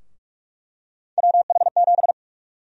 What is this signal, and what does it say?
Morse code, heard as a keyed tone, about 46 wpm, 700 Hz. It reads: WH7
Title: USA Call Sign Prefixes - 45wpm